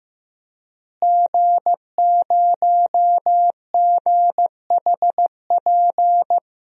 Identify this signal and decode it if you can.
Morse code, heard as a keyed tone, about 15 wpm, 700 Hz. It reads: G0GHP
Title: DXCC-100 Call Signs - 5 Characters - Repeated 2x - 15wpm